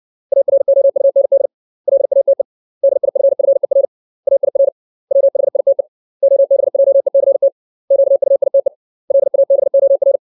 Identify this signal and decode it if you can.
Morse code, heard as a keyed tone, about 64 wpm, 550 Hz. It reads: AROUND LATE DIFFER AIR WHITE OBJECT QUITE LABOR
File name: 632 words / 632 Words 60WPM